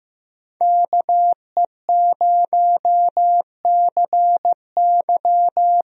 Morse code, heard as a keyed tone, about 15 wpm, 700 Hz. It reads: KE0CY
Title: USA Call Signs - 5 Characters - Repeated 2x - 15wpm